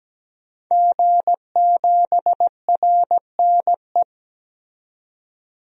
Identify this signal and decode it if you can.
Morse code, heard as a keyed tone, about 17 wpm, 700 Hz. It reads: G7RNE
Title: DXCC-100 Call Signs - 5 Characters - 17wpm